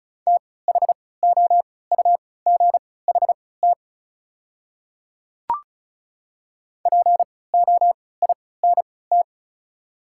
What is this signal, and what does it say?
Morse code, heard as a keyed tone, about 34 wpm, spaced as for 20 wpm, 700 Hz. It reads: THOUGHT POINT
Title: 3x Character Spacing - Top 300 Words - 35wpm